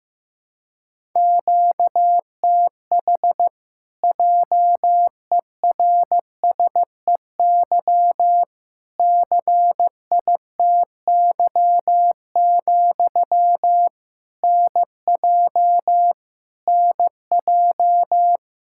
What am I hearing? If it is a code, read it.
Morse code, heard as a keyed tone, about 15 wpm, 700 Hz. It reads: QTH JERSEY CITY, NJ NJ
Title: Practice QSO - Exchange 1 - Beginning - 15wpm